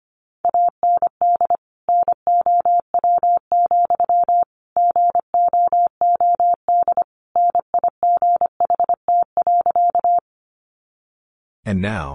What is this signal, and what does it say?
Morse code, heard as a keyed tone, about 25 wpm, 700 Hz. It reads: AND NOW, GOOD NIGHT.
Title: Sentences from Top 300 Words 25wpm